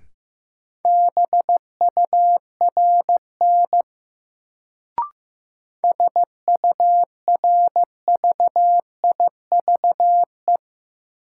Morse code, heard as a keyed tone, about 15 wpm, 700 Hz. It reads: BURN SURVIVE
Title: Sentences from Top 1300 Words - New Word Drill - 15wpm